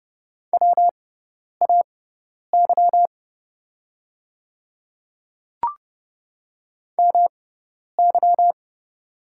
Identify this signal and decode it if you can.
Morse code, heard as a keyed tone, about 30 wpm, spaced as for 10 wpm, 690 Hz. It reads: WAY MY
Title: 6x Character Spacing - Top 100 Words - 30wpm